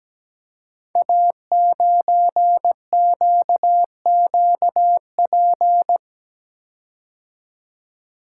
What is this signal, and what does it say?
Morse code, heard as a keyed tone, about 17 wpm, 700 Hz. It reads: A9QQP
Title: Code Groups - 5 Letter:Numbers - 17wpm